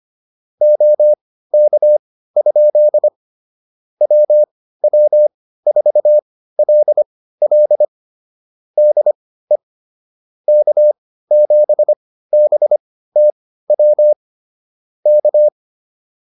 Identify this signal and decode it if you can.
Morse code, heard as a keyed tone, about 25 wpm, spaced as for 15 wpm, 600 Hz. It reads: OK? WW4LL DE K7BTW K